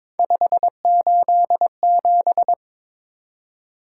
Morse code, heard as a keyed tone, about 22 wpm, 700 Hz. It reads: 587
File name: practice / 22wpm / Three Letter-Number Combinations - 22wpm